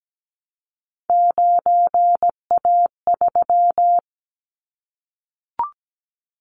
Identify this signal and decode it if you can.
Morse code, heard as a keyed tone, about 17 wpm, 700 Hz. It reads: 9A3
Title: Callsign Prefixes for Top 25 DXCC - Repeated 3x - 17wpm